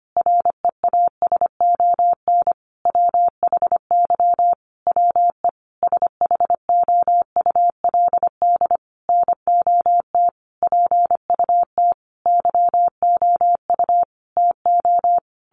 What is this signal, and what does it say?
Morse code, heard as a keyed tone, about 25 wpm, 700 Hz. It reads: REASON WHY WE SHOULD NOT PUT YOU TO
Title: Sentences from Top 1100 Words 25wpm